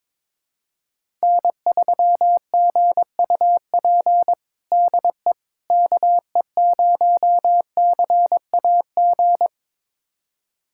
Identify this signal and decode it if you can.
Morse code, heard as a keyed tone, about 22 wpm, 700 Hz. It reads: N3GUP DE KE0CAG
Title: USA Callsign DE Callsign 22wpm